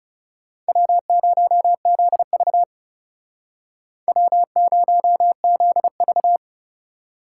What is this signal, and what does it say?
Morse code, heard as a keyed tone, about 34 wpm, 700 Hz. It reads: W0ZV W0ZV
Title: Speed Racing - USA 1x2 Call Signs 30wpm